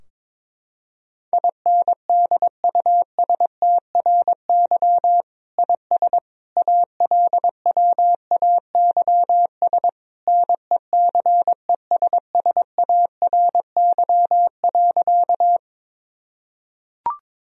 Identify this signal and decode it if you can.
Morse code, heard as a keyed tone, about 22 wpm, 700 Hz. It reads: INDUSTRY IS ALWAYS NECESSARY.
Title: Sentences from Top 1000 Words - Review Part 4 - 22wpm